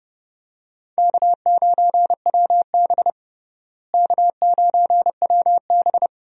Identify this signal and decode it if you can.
Morse code, heard as a keyed tone, about 30 wpm, 700 Hz. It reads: K9WB K9WB
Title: USA Call Signs - 4 Characters - Repeated 2x - 30wpm